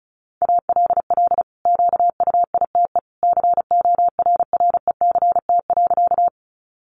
Morse code, heard as a keyed tone, about 34 wpm, 700 Hz. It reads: ALL QUITE CORRECT.
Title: Sentences from Top 700 Words 35wpm